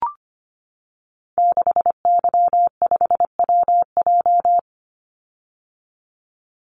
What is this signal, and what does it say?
Morse code, heard as a keyed tone, about 25 wpm, 700 Hz. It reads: 6Y5WJ
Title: Call Signs - 25wpm